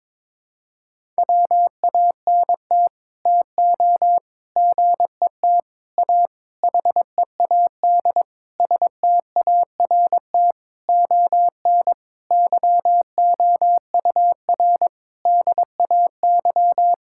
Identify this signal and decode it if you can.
Morse code, heard as a keyed tone, about 22 wpm, 700 Hz. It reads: WANT TO GET A HEAD START ON YOUR DAY
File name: Sets of 9 Words - 22wpm